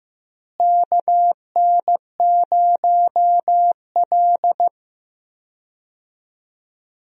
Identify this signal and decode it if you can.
Morse code, heard as a keyed tone, about 15 wpm, 700 Hz. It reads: KN0L
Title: USA Call Signs - 4 Characters - Repeated 2x - 15wpm